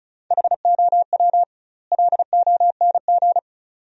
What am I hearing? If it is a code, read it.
Morse code, heard as a keyed tone, about 36 wpm, 700 Hz. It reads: HOW LONG
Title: 2x Word-Spacing - Sentences from Top 200 Words - 35wpm